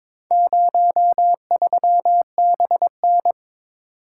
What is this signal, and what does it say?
Morse code, heard as a keyed tone, about 22 wpm, 700 Hz. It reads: 03BN
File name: practice / 22wpm / Four Letter-Number Combinations - 22wpm